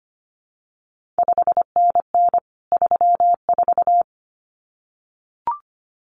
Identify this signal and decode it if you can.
Morse code, heard as a keyed tone, about 25 wpm, 700 Hz. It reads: 5NN 34
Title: CQ WW DX Contest Exchange Encore - 25wpm